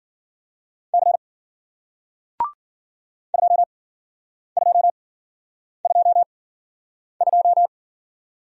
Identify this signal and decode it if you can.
Morse code, heard as a keyed tone, about 54 wpm, spaced as for 9 wpm, 700 Hz. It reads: C 2222